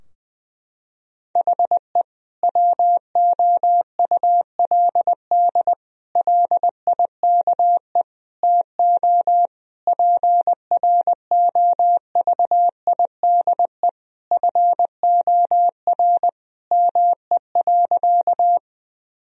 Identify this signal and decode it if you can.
Morse code, heard as a keyed tone, about 20 wpm, 700 Hz. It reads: HE WOULD LIKE TO PROVIDE FOR ME.
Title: Sentences from Top 1000 Words - Review Part 3 - 20wpm